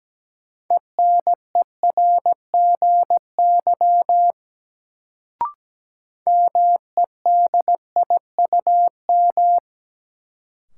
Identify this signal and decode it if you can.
Morse code, heard as a keyed tone, about 17 wpm, 700 Hz. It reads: ENERGY MEDIUM
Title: Sentences from Top 900 Words - New Word Drill - 17wpm